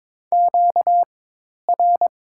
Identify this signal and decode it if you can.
Morse code, heard as a keyed tone, about 22 wpm, spaced as for 10 wpm, 700 Hz. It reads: QR
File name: Top 90 QSO Elements - 4x Character Spacing - 22wpm